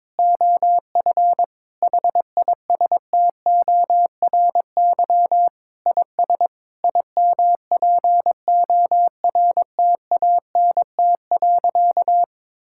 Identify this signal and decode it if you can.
Morse code, heard as a keyed tone, about 21 wpm, 700 Hz. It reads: OF HISTORY IS IMPORTANT.
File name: practice / 22wpm / Sentences with 8-Word Phrases - 22wpm